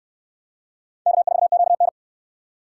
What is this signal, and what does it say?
Morse code, heard as a keyed tone, about 69 wpm, 700 Hz. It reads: K4QN